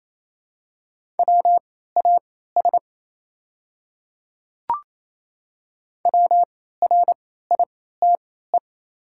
Morse code, heard as a keyed tone, about 27 wpm, spaced as for 16 wpm, 690 Hz. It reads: WAS WRITE
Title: 3x Character Spacing - Top 100 Words - 28wpm